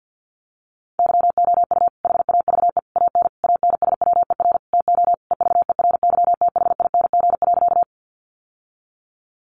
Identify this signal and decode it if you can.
Morse code, heard as a keyed tone, about 50 wpm, 700 Hz. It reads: YOU HAVE AN ANSWER TO EVERYTHING.
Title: Sentences from Top 700 Words 50wpm